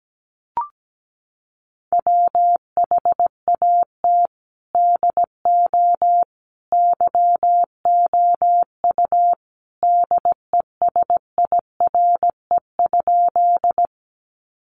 Morse code, heard as a keyed tone, about 17 wpm, 700 Hz. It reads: WHAT DO YOU DESIRE?
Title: Sentences from Top 1100 Words 17wpm